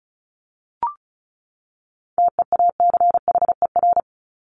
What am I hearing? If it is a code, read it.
Morse code, heard as a keyed tone, about 34 wpm, 700 Hz. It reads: TEACHER
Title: Top 751-1000 Words 35wpm